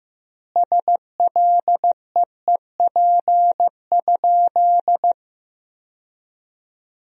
Morse code, heard as a keyed tone, about 15 wpm, 700 Hz. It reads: SLEEP?
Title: 2x Word-Spacing - Sentences from Top 500 Words - 15wpm